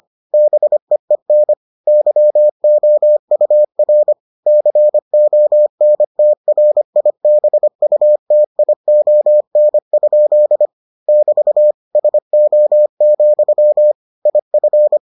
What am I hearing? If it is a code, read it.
Morse code, heard as a keyed tone, about 25 wpm, 600 Hz. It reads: BEEN YOUR CONTRIBUTION? <BT> SO, IF